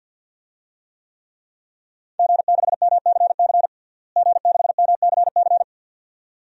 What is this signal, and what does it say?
Morse code, heard as a keyed tone, about 46 wpm, 700 Hz. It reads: G6MCC G6MCC
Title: DXCC-100 Call Signs - 5 Characters - Repeated 2x - 50wpm